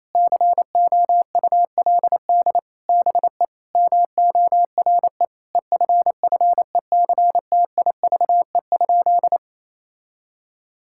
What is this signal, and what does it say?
Morse code, heard as a keyed tone, about 28 wpm, 700 Hz. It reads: COULD BE MORE EFFECTIVE?
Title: Sentences from Top 1000 Words - Review Part 6 - 28wpm